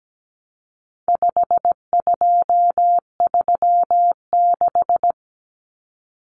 Morse code, heard as a keyed tone, about 17 wpm, 700 Hz. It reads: 5236
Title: Numbers - 4-Digit 17wpm